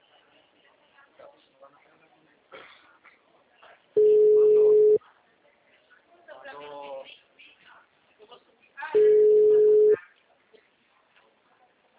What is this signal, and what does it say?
Ringback tone (Europe, 425 Hz), with speech.